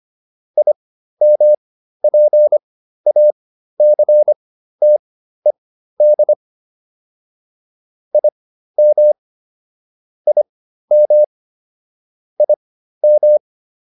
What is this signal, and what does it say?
Morse code, heard as a keyed tone, about 25 wpm, spaced as for 13 wpm, 600 Hz. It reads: IMPACTED IM IM IM